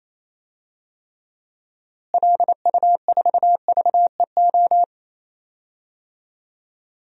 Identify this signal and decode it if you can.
Morse code, heard as a keyed tone, about 28 wpm, 700 Hz. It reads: LU4VEO
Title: DXCC-100 Call Signs - 6 Characters - Repeated 2x - 28wpm